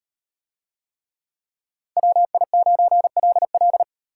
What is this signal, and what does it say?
Morse code, heard as a keyed tone, about 35 wpm, 700 Hz. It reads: WI9LL